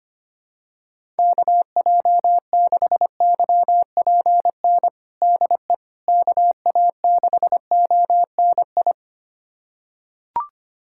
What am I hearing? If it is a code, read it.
Morse code, heard as a keyed tone, about 25 wpm, 700 Hz. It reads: KJ6YPN DE KA6ONI
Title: USA Callsign DE Callsign - 6 Characters - 25wpm